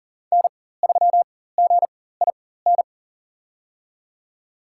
Morse code, heard as a keyed tone, about 39 wpm, spaced as for 18 wpm, 700 Hz. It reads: N3GIN
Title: USA Call Signs - 5 Characters - 4x Character Spacing - 40wpm